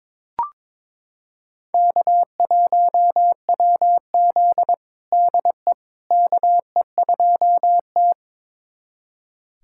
Morse code, heard as a keyed tone, about 22 wpm, 700 Hz. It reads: K1WZ DE KE2T
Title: USA Callsign DE Callsign - 4 Characters - 22wpm